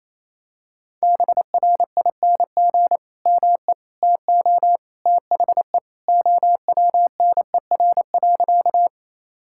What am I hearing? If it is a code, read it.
Morse code, heard as a keyed tone, about 28 wpm, 700 Hz. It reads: BRING ME TO THE OWNER.
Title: Sentences from Top 1600 Words 28wpm